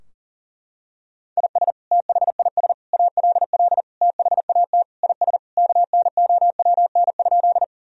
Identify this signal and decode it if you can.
Morse code, heard as a keyed tone, about 40 wpm, 700 Hz. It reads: IS THIS ALL THAT IS KNOWN?
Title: Sentences from Top 1000 Words - Review Part 3 - 40wpm